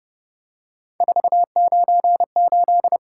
Morse code, heard as a keyed tone, about 29 wpm, 700 Hz. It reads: 498
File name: Speed-Racing - Three Letter-Number Combinations - 25wpm